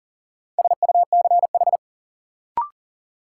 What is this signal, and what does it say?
Morse code, heard as a keyed tone, about 40 wpm, 700 Hz. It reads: SUCH
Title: Sets of 2 Words - New Word Drill - 40wpm